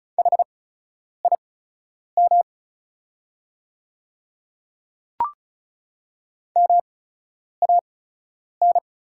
Morse code, heard as a keyed tone, about 36 wpm, spaced as for 10 wpm, 690 Hz. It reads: HIM MAN
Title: 8x Character Spacing - Top 300 Words - 35wpm